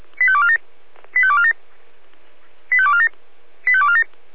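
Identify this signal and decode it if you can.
Five-tone selective call (ZVEI), heard as quick runs of five tones, tones of about 70 ms. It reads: 85247 85247 85247 85247
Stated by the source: ZVEI five-tone selective call received on SDR, address 85218.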